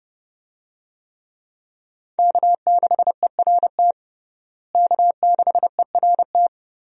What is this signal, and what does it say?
Morse code, heard as a keyed tone, about 32 wpm, 700 Hz. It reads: K6ERT K6ERT
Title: USA Call Signs - 5 Characters - Repeated 3x - 30wpm